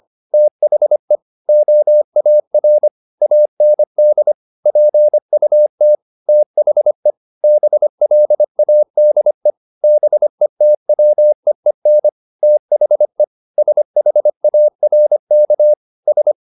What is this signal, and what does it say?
Morse code, heard as a keyed tone, about 25 wpm, 600 Hz. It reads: THE OAR AND PUT THE BLADE BETWEEN THE SHARK S